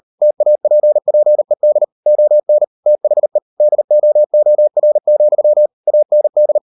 Morse code, heard as a keyed tone, about 38 wpm, 600 Hz. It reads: TAPPED ON THE DOOR, AND